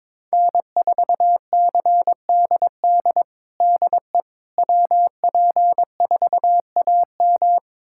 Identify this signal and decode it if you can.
Morse code, heard as a keyed tone, about 22 wpm, 700 Hz. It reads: N4CDD DE WP4AM
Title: USA Callsign DE Callsign - 5 Characters - 22wpm